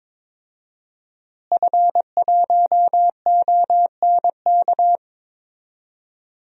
Morse code, heard as a keyed tone, about 22 wpm, 700 Hz. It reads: F1ONK
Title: DXCC-100 Call Signs - 5 Characters - Repeated 3x - 22wpm